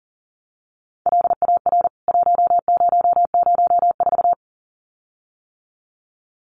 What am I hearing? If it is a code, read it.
Morse code, heard as a keyed tone, about 37 wpm, 700 Hz. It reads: LAR 1004
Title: CWT Exchange 40wpm